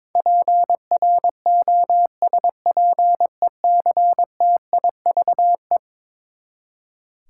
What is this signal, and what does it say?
Morse code, heard as a keyed tone, about 22 wpm, 700 Hz. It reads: PROSPECTIVE
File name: practice / 22wpm / Eleven Letter Words - 22wpm